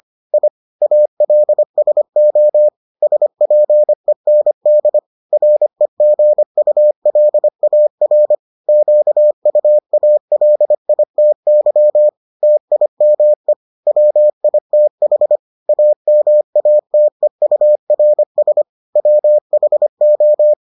Morse code, heard as a keyed tone, about 25 wpm, 600 Hz. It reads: I ALSO SPEND REGULAR QUALITY TIME WITH AMATEURS WHO